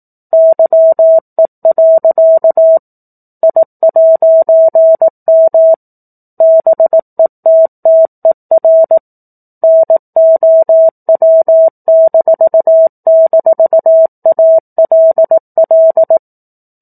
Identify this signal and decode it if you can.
Morse code, heard as a keyed tone, about 18 wpm, 650 Hz. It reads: YE. I'M BETTER NOW--ALL
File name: 04-Chapter-04 18 WPM